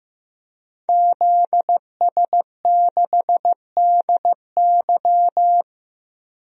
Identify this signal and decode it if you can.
Morse code, heard as a keyed tone, about 15 wpm, 700 Hz. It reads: ZS6DY
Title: DXCC-100 Call Signs - 5 Characters - Repeated 2x - 15wpm